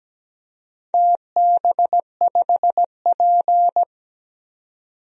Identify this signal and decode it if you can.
Morse code, heard as a keyed tone, about 17 wpm, 700 Hz. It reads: TB5P
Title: DXCC-100 Call Signs - 4 Characters - 17wpm